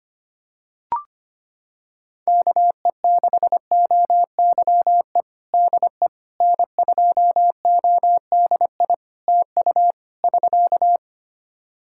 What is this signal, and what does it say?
Morse code, heard as a keyed tone, about 25 wpm, 700 Hz. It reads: KE6OYE DE N2ODI TU <SK>
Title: K1USN SST QSO Practice 25wpm